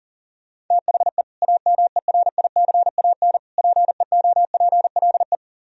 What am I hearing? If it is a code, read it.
Morse code, heard as a keyed tone, about 51 wpm, 700 Hz. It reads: THE AMERICAN PEOPLE